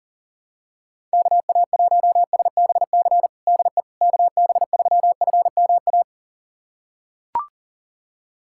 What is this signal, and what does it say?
Morse code, heard as a keyed tone, about 37 wpm, 700 Hz. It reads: KA1SBC DE KB3FMA
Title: USA Callsign DE Callsign - 6 Characters - 40wpm